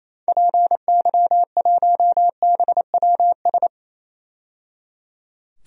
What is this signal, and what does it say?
Morse code, heard as a keyed tone, about 27 wpm, 700 Hz. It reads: PY1BWS
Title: DXCC-100 Call Signs - 6 Characters - 28wpm